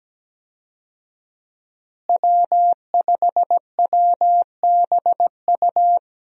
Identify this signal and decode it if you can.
Morse code, heard as a keyed tone, about 17 wpm, 700 Hz. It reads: W5WBU
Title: USA Call Signs - 5 Characters - Repeated 2x - 17wpm